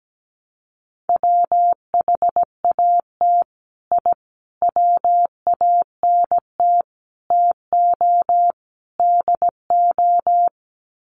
Morse code, heard as a keyed tone, about 17 wpm, 700 Hz. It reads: WHAT I WANT TO DO
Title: Sets of 5 Words Encore 17wpm